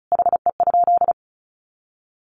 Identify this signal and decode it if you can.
Morse code, heard as a keyed tone, about 34 wpm, 700 Hz. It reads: HE?
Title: Sentences from Top 1300 Words 35wpm